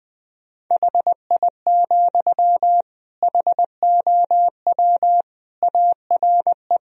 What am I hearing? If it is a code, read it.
Morse code, heard as a keyed tone, about 20 wpm, 700 Hz. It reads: HI, HOW ARE
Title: Sentences from Top 1000 Words - Review Part 3 - 20wpm